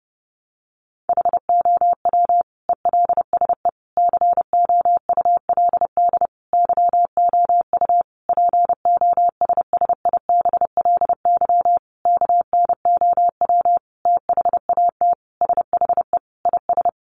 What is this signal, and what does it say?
Morse code, heard as a keyed tone, about 28 wpm, 700 Hz. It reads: HOW ELSE COULD YOU POSSIBLY KNOW THAT SHE IS
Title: Sentences from Top 1300 Words 30wpm